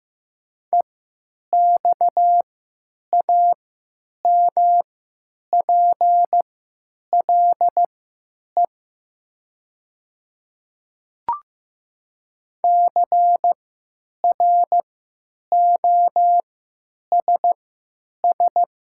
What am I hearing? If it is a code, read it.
Morse code, heard as a keyed tone, about 15 wpm, spaced as for 9 wpm, 700 Hz. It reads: EXAMPLE CROSS
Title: 3x Character Spacing - Top 500 Words - 15wpm